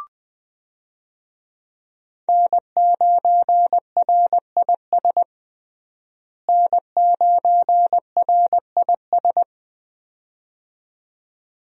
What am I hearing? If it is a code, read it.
Morse code, heard as a keyed tone, about 20 wpm, 700 Hz. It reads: N9RIS N9RIS
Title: USA Call Signs - 5 Characters - Repeated 2x - 20wpm